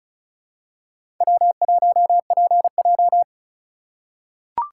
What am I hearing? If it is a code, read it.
Morse code, heard as a keyed tone, about 34 wpm, 700 Hz. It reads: W1PJ
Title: USA Call Signs - 3x Character Spacing - 35wpm